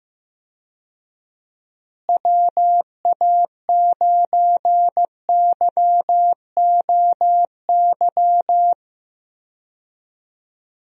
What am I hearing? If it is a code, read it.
Morse code, heard as a keyed tone, about 15 wpm, 700 Hz. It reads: WA9YOY
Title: USA Call Signs - 6 Characters - Repeated 2x - 15wpm